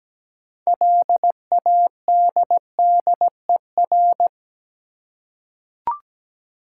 Morse code, heard as a keyed tone, about 17 wpm, 700 Hz. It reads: LADDER
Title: Six Letter Words - 17wpm